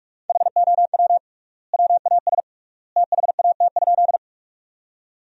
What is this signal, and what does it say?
Morse code, heard as a keyed tone, about 41 wpm, 700 Hz. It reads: HOW WAS THAT?
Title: 3x Word-Spacing - Sentences from Top 100 Words - 45wpm